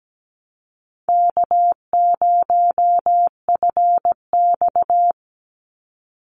Morse code, heard as a keyed tone, about 17 wpm, 700 Hz. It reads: K0FX
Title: Call Signs - 17wpm